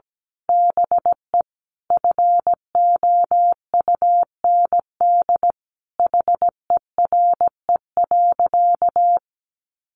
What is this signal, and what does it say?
Morse code, heard as a keyed tone, about 17 wpm, 700 Hz. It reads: BE FOUND HERE.
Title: Sentences from Top 1500 Words 17wpm